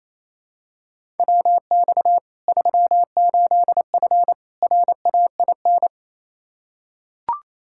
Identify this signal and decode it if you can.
Morse code, heard as a keyed tone, about 28 wpm, 700 Hz. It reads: WX 38F RAIN
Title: Practice QSO - Exchange 2 - Info - 28wpm